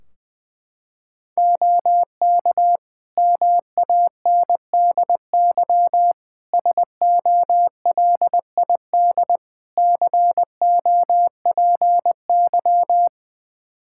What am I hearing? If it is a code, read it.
Morse code, heard as a keyed tone, about 21 wpm, 700 Hz. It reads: OK MANDY SOLID COPY
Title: Practice QSO - Exchange 3 - More Info - 20wpm